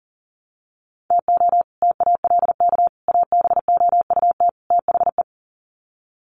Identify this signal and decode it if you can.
Morse code, heard as a keyed tone, about 40 wpm, 700 Hz. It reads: TO TALK ABOUT THE
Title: Sets of 4 Words 40wpm